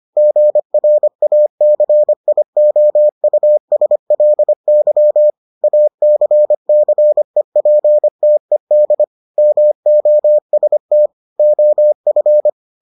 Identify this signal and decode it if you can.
Morse code, heard as a keyed tone, about 25 wpm, 600 Hz. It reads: GRACIOUSLY ACCEPTED MOST OF